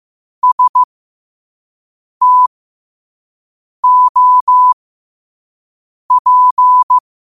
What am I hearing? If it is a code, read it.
Morse code, heard as a keyed tone, about 15 wpm, spaced as for 5 wpm, 1000 Hz. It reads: STOP
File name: qotdfiles / 5wpm